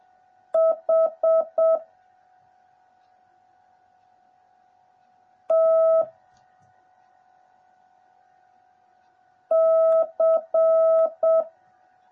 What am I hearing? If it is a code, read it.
Morse code, heard as a keyed tone, about 7 wpm, 650 Hz. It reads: H T C
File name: jh8cbh / 2021cw / cw09a